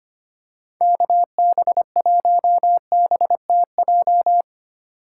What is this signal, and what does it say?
Morse code, heard as a keyed tone, about 25 wpm, 700 Hz. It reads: KB1BTJ
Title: USA Call Signs - 2x Character Spacing - 25wpm